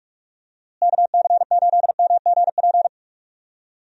Morse code, heard as a keyed tone, about 45 wpm, 700 Hz. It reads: KC8MGP
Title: USA Call Signs - 6 Characters - Repeated 2x - 45wpm